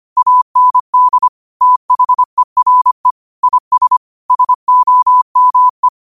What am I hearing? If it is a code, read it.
Morse code, heard as a keyed tone, about 25 wpm, 1000 Hz. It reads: AND THERE IS SOME
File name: qotdfiles / 25wpm